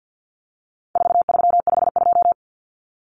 Morse code, heard as a keyed tone, about 50 wpm, 700 Hz. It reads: 4352
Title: Common Cut Numbers - 4-Digit 50wpm